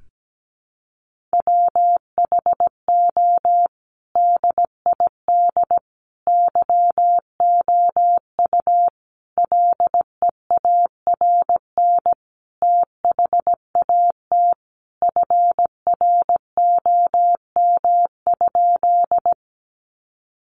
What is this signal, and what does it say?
Morse code, heard as a keyed tone, about 17 wpm, 700 Hz. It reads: WHO DID YOU LEARN THAT FROM?
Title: Sentences from Top 300 Words 17wpm